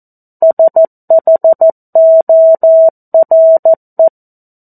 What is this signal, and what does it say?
Morse code, heard as a keyed tone, about 14 wpm, 650 Hz. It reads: SHORE
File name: 02-Chapter-02 14 WPM